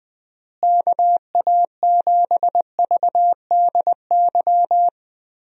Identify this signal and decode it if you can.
Morse code, heard as a keyed tone, about 20 wpm, 700 Hz. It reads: KA7VDY
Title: USA Call Signs - 6 Characters - Repeated 2x - 20wpm